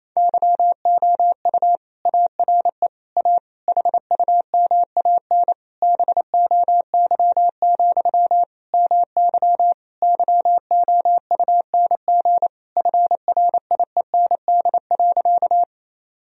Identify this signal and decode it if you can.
Morse code, heard as a keyed tone, about 28 wpm, 700 Hz. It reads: YOU ARE A HUMAN BOY, MY YOUNG FRIEND.
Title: Sentences from Top 1000 Words - Review Part 3 - 28wpm